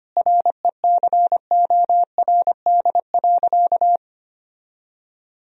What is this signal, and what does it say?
Morse code, heard as a keyed tone, about 25 wpm, 700 Hz. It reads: RECORD.
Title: Sentences from Top 1000 Words - Review Part 3 - 25wpm